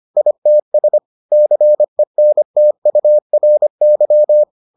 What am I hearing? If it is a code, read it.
Morse code, heard as a keyed tone, about 25 wpm, 600 Hz. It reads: ITS CENTURY